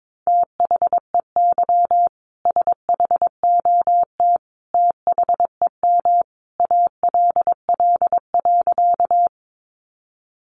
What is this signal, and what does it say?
Morse code, heard as a keyed tone, about 22 wpm, 700 Hz. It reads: THEY SHOT THEM ALL.
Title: Sentences from Top 1100 Words 22wpm